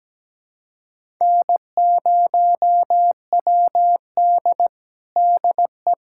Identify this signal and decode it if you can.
Morse code, heard as a keyed tone, about 17 wpm, 700 Hz. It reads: N0WD DE
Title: USA Callsign DE Callsign - 4 Characters - 17wpm